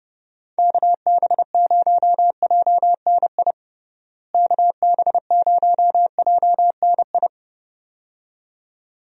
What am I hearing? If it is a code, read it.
Morse code, heard as a keyed tone, about 32 wpm, 700 Hz. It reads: KB0JNI KB0JNI
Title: USA Call Signs - 6 Characters - Repeated 2x - 30wpm